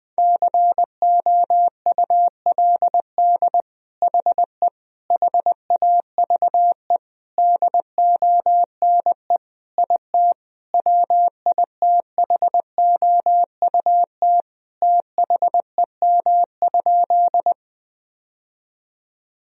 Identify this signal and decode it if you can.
Morse code, heard as a keyed tone, about 20 wpm, 700 Hz. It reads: COULD HE HAVE DONE IT WITHOUT THEM?
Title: Sentences from Top 1000 Words - Review Part 3 - 20wpm